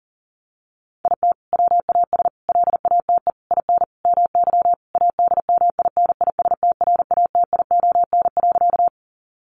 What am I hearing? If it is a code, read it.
Morse code, heard as a keyed tone, about 40 wpm, 700 Hz. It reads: IT WAS LATE IN MY ADMINISTRATION.
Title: Sentences from Top 1300 Words 40wpm